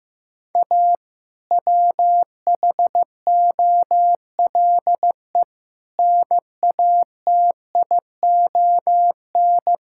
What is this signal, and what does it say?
Morse code, heard as a keyed tone, about 15 wpm, 700 Hz. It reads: A WHOLE NATION
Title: Sentences from Top 1000 Words - Review Part 6 - 15wpm